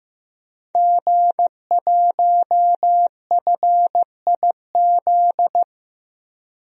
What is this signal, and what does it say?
Morse code, heard as a keyed tone, about 15 wpm, 700 Hz. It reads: G1FIZ
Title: DXCC-100 Call Signs - 5 Characters - 15wpm